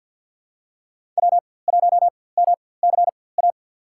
Morse code, heard as a keyed tone, about 48 wpm, spaced as for 23 wpm, 700 Hz. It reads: W1MCA
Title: USA Call Signs - 5 Characters - 4x Character Spacing - 50wpm